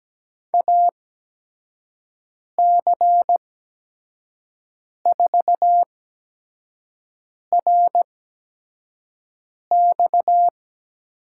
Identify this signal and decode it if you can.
Morse code, heard as a keyed tone, about 17 wpm, spaced as for 5 wpm, 700 Hz. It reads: AC4RX
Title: USA Call Signs - 5 Characters - 8x Character Spacing - 17wpm